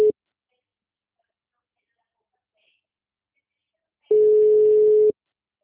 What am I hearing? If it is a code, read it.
Ringback tone (Europe, 425 Hz), with speech.